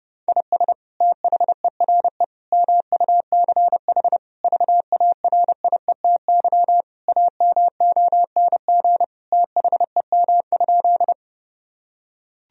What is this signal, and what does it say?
Morse code, heard as a keyed tone, about 30 wpm, 700 Hz. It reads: IS THERE MUCH VARIETY AMONG THEM?
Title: Sentences from Top 1000 Words - Review Part 4 - 30wpm